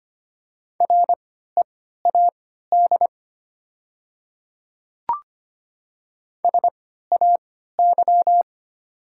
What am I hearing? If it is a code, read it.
Morse code, heard as a keyed tone, about 25 wpm, spaced as for 14 wpm, 700 Hz. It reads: READ SAY
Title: 3x Character Spacing - Top 200 Words - 25wpm